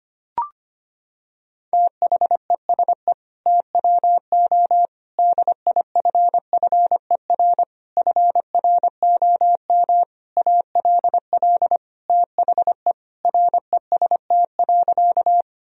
Morse code, heard as a keyed tone, about 25 wpm, 700 Hz. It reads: THESE TWO DIFFER FROM ALL THE REST.
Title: Sentences from Top 1000 Words - Review Part 1 - 25wpm